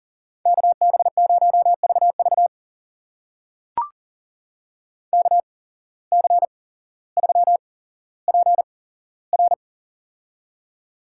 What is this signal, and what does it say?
Morse code, heard as a keyed tone, about 40 wpm, spaced as for 17 wpm, 700 Hz. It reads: KB0VV KC3PR
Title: USA Call Signs - 5 Characters - 8x Character Spacing - 40wpm